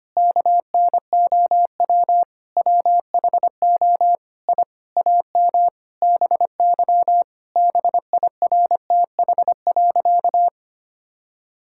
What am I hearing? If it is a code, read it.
Morse code, heard as a keyed tone, about 25 wpm, 700 Hz. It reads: KNOW WHO I AM BY BIRTH.
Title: Sentences from Top 1000 Words - Review Part 5 - 25wpm